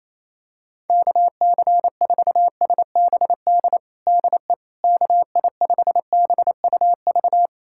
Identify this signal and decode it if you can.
Morse code, heard as a keyed tone, about 28 wpm, 700 Hz. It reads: KC4SBD DE KI5BUV
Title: USA Callsign DE Callsign - 6 Characters - 28wpm